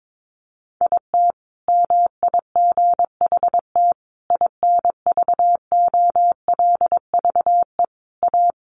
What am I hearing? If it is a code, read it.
Morse code, heard as a keyed tone, about 22 wpm, 700 Hz. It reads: IT MIGHT INVOLVE A
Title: Sentences from Top 1200 Words 22wpm